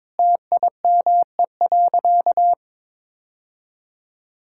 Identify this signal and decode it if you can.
Morse code, heard as a keyed tone, about 22 wpm, 700 Hz. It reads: TIME.
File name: practice / 22wpm / Sentences with 3-Word Phrases - 22wpm